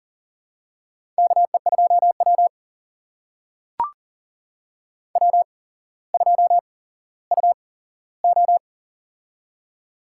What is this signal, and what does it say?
Morse code, heard as a keyed tone, about 40 wpm, spaced as for 17 wpm, 700 Hz. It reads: KE2W W2UO
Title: USA Call Signs - 4 Characters - 8x Character Spacing - 40wpm